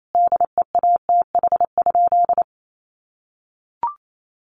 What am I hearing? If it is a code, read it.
Morse code, heard as a keyed tone, about 28 wpm, 700 Hz. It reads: DEATH?
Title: Sentences from Top 1500 Words 28wpm